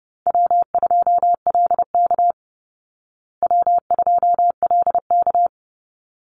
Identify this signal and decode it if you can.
Morse code, heard as a keyed tone, about 30 wpm, 700 Hz. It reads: W2LK W2LK
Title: Easy Call Signs Repeated 2x - 30wpm